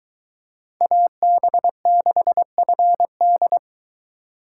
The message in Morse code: AB6FD

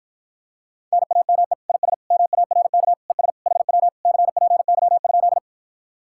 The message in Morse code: NAME IS MARK ES HW CPY?

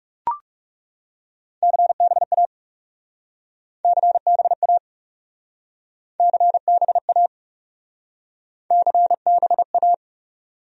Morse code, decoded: CBA CBA CBA CBA

